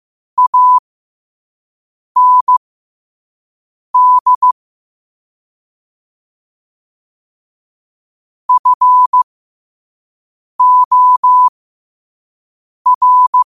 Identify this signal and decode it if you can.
Morse code, heard as a keyed tone, about 15 wpm, spaced as for 5 wpm, 1000 Hz. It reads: AND FOR
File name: qotdfiles / 5wpm